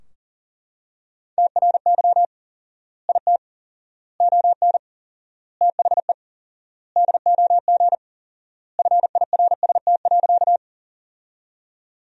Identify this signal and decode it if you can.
Morse code, heard as a keyed tone, about 40 wpm, 700 Hz. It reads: TRY IT ON THE DOG FIRST.
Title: 4x Word-Spacing - Sentences from Top 400 Words - 40wpm